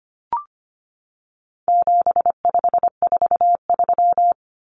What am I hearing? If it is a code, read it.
Morse code, heard as a keyed tone, about 25 wpm, 700 Hz. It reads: 7543
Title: Common Cut Numbers - 4-Digit 25wpm